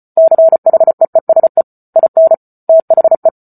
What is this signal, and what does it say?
Morse code, heard as a keyed tone, about 34 wpm, 650 Hz. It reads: CHEESE IN THE